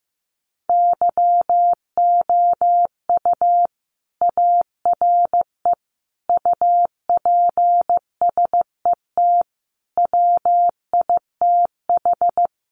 Morse code, heard as a keyed tone, about 15 wpm, 700 Hz. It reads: YOU ARE UPSET WITH
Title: Sentences from Top 800 Words 15wpm